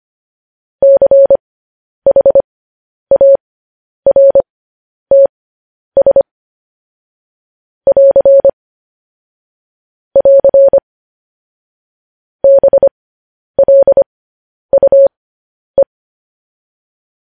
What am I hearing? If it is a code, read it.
Morse code, heard as a keyed tone, about 25 wpm, spaced as for 10 wpm, 550 Hz. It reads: CHARTS <AR> <AR> BLUE
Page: news-headlines-in-morse-code-at-10-wpm